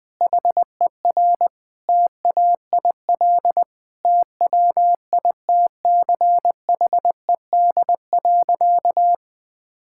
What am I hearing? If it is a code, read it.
Morse code, heard as a keyed tone, about 20 wpm, 700 Hz. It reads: HER TAIL TWITCHED.
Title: Sentences with 2-Word Phrases - 20wpm